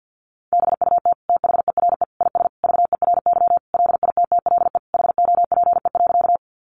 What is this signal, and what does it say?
Morse code, heard as a keyed tone, about 50 wpm, 700 Hz. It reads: BUT THERE IS VERY LITTLE HOPE.
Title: Sentences from Top 1000 Words 50wpm